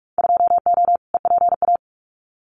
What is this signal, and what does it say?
Morse code, heard as a keyed tone, about 44 wpm, 700 Hz. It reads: 2O EPA
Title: Winter Field Day Exchanges Encore 45wpm